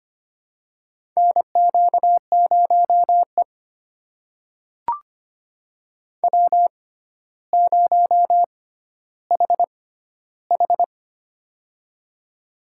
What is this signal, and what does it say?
Morse code, heard as a keyed tone, about 25 wpm, spaced as for 13 wpm, 700 Hz. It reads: NQ0E W0HH
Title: USA Call Signs - 4 Characters - 6x Character Spacing - 25wpm